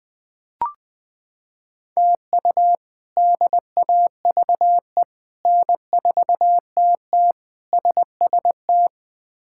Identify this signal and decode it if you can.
Morse code, heard as a keyed tone, about 20 wpm, 700 Hz. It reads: TU DAVE N4TT SST
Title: K1USN SST QSO Practice - 20wpm